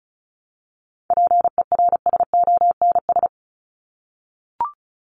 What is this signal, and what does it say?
Morse code, heard as a keyed tone, about 34 wpm, 700 Hz. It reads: PERSONS
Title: ES:S Suffix - Top 100 Words with Base Words 35wpm